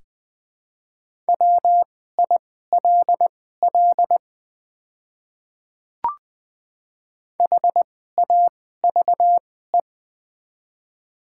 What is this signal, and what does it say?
Morse code, heard as a keyed tone, about 20 wpm, spaced as for 14 wpm, 690 Hz. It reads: WILL HAVE
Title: 2x Character Spacing - Top 100 Words - 20wpm